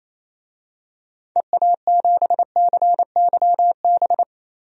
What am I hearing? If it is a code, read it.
Morse code, heard as a keyed tone, about 28 wpm, 700 Hz. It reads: EA7CYB
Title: DXCC-100 Call Signs - 6 Characters - Repeated 3x - 28wpm